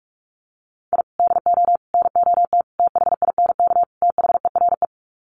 Morse code, heard as a keyed tone, about 44 wpm, 700 Hz. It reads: I DO NOT THINK THERE
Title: Sets of 5 Words 45wpm